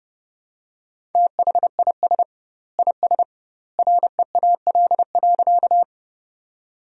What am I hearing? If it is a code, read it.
Morse code, heard as a keyed tone, about 32 wpm, 700 Hz. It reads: THIS IS REAL.
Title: 2x Word-Spacing - Sentences from Top 300 Words - 30wpm